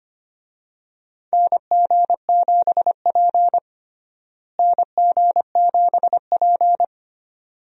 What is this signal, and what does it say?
Morse code, heard as a keyed tone, about 25 wpm, 700 Hz. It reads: NG7P NG7P
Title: USA Call Signs - 4 Characters - Repeated 2x - 25wpm